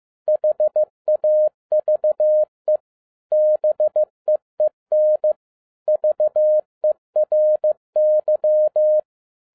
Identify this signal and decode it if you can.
Morse code, heard as a keyed tone, about 15 wpm, 610 Hz. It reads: HAVE BEEN VERY
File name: 2011-07-mp3 / n42 EXCITING QRP CW OPERATION 15wpm